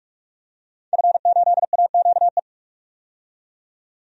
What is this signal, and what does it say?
Morse code, heard as a keyed tone, about 45 wpm, 700 Hz. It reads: F8AQE
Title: DXCC-100 Call Signs - 5 Characters - 45wpm